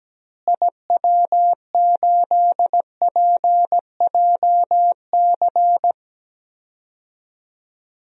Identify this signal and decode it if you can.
Morse code, heard as a keyed tone, about 17 wpm, 700 Hz. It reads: IW8PJC
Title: DXCC-100 Call Signs - 6 Characters - Repeated 2x - 17wpm